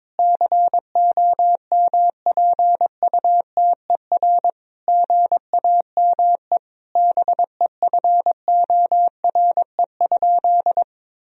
Morse code, heard as a keyed tone, about 22 wpm, 700 Hz. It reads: COMPUTER GAME BEFORE?
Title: Sentences from Top 1000 Words - Review Part 3 - 22wpm